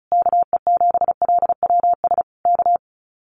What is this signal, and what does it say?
Morse code, heard as a keyed tone, about 34 wpm, 700 Hz. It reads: KE7LWS K
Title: Common QSO Phrases Encore 35wpm